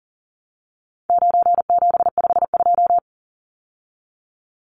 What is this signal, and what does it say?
Morse code, heard as a keyed tone, about 40 wpm, 700 Hz. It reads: 9752
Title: Numbers - 4-Digit 40wpm